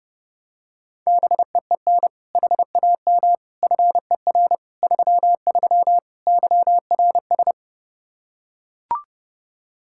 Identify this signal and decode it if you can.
Morse code, heard as a keyed tone, about 30 wpm, 700 Hz. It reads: BEEN HAM FER 33 YRS
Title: Practice QSO - Exchange 3 - More Info - 30wpm